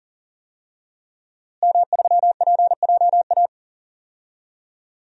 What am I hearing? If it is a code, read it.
Morse code, heard as a keyed tone, about 37 wpm, 700 Hz. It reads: M3PJA
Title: DXCC-100 Call Signs - 5 Characters - Repeated 2x - 40wpm